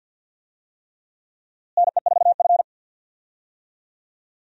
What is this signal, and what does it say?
Morse code, heard as a keyed tone, about 52 wpm, 700 Hz. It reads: NE4F
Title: USA Call Signs - 4 Characters - Repeated 3x - 50wpm